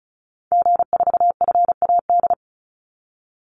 Morse code, heard as a keyed tone, about 34 wpm, 700 Hz. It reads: G4FAD
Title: Call Signs Repeated 3x - 35wpm